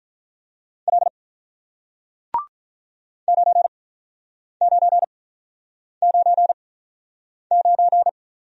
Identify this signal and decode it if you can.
Morse code, heard as a keyed tone, about 48 wpm, 700 Hz. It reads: L 9 9 9 9